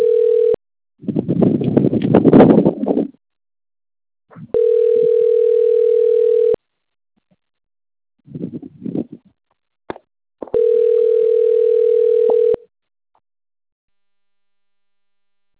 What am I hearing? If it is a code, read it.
Ringback tone (North America), 440 + 480 Hz.